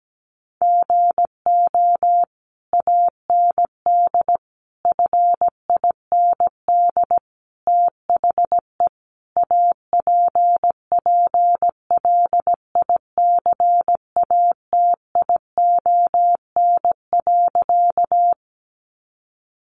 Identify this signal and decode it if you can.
Morse code, heard as a keyed tone, about 17 wpm, 700 Hz. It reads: GO AND FIND THE APPLICATION.
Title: Sentences from Top 1200 Words 17wpm